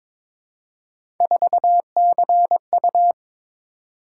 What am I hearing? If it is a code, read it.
Morse code, heard as a keyed tone, about 22 wpm, 700 Hz. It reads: 4CU
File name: practice / 22wpm / Three Letter-Number Combinations - 22wpm